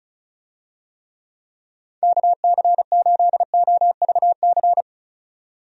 Morse code, heard as a keyed tone, about 36 wpm, 700 Hz. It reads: KC8OVC
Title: USA Call Signs - 6 Characters - 35wpm